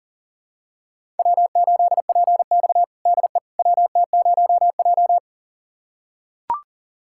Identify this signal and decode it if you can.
Morse code, heard as a keyed tone, about 40 wpm, 700 Hz. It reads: W8PX DE WT0J
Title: USA Callsign DE Callsign - 4 Characters - 40wpm